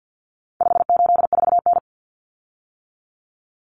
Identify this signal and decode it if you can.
Morse code, heard as a keyed tone, about 48 wpm, 700 Hz. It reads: 584N
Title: Common Cut Numbers - 4-Digit 50wpm